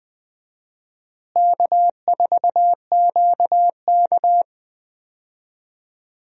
Morse code, heard as a keyed tone, about 20 wpm, 700 Hz. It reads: K4QK
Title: USA Call Signs - 4 Characters - Repeated 3x - 20wpm